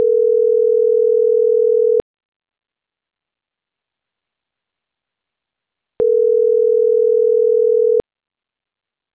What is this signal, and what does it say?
Ringback tone (North America), 440 + 480 Hz.